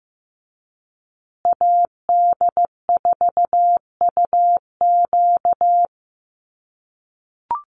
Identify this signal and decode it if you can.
Morse code, heard as a keyed tone, about 15 wpm, 700 Hz. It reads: AD4UQ
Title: USA Call Signs Repeated 3x 15wpm